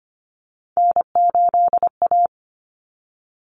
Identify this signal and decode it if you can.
Morse code, heard as a keyed tone, about 25 wpm, 700 Hz. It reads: N8A
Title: Common Cut Numbers - 3-Digit 25wpm